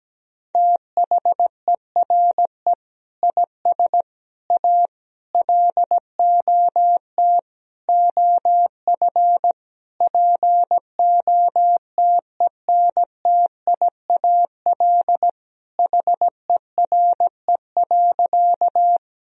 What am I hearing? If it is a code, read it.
Morse code, heard as a keyed tone, about 17 wpm, 700 Hz. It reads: THERE IS A LOT OF POTENTIAL HERE.
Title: Sentences from Top 1000 Words - Review Part 4 - 17wpm